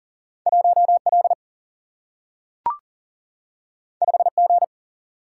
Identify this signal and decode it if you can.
Morse code, heard as a keyed tone, about 41 wpm, 700 Hz. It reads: 1L 5G
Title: Single-Letter Single-Number 40wpm